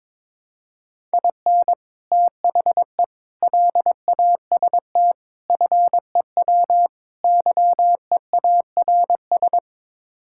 Morse code, heard as a keyed tone, about 22 wpm, 700 Hz. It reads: IN THE LAST FEW YEARS
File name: practice / 22wpm / Build A Phrase - 5-Word Phrases - 22wpm